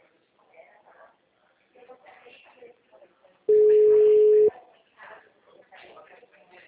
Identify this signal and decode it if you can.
Ringback tone (Europe, 425 Hz), with speech.
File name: out- -2004- - - .40240